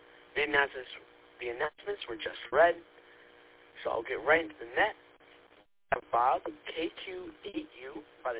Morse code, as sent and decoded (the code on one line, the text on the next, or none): none